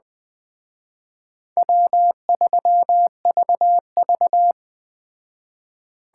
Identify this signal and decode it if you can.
Morse code, heard as a keyed tone, about 21 wpm, 700 Hz. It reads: W3VV
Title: Speed Racing - USA 1x2 Call Signs 17wpm